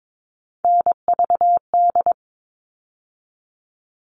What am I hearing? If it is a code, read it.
Morse code, heard as a keyed tone, about 22 wpm, 700 Hz. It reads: NVD
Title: All Cut Numbers - 3-Digit 22wpm